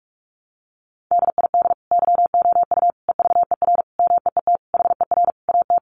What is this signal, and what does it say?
Morse code, heard as a keyed tone, about 44 wpm, 700 Hz. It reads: DID YOU EVER MEET HER AT
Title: Sentences from Top 1100 Words 45wpm